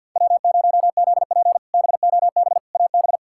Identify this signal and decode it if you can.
Morse code, heard as a keyed tone, about 53 wpm, 700 Hz. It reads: W0ZP BOB AB
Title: K1USN SST QSO Practice - 50wpm